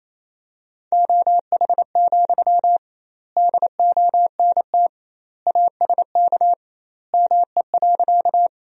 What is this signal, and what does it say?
Morse code, heard as a keyed tone, about 28 wpm, 700 Hz. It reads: OH, DONT ASK ME.
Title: 2x Word-Spacing - Sentences from Top 500 Words - 28wpm